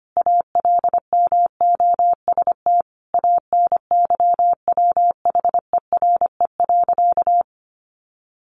Morse code, heard as a keyed tone, about 25 wpm, 700 Hz. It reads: ALMOST ANYWHERE.
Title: Sentences from Top 1400 Words 25wpm